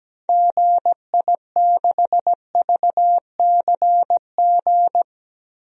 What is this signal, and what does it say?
Morse code, heard as a keyed tone, about 17 wpm, 700 Hz. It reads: GI6VCG
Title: DXCC-100 Call Signs - 6 Characters - Repeated 2x - 17wpm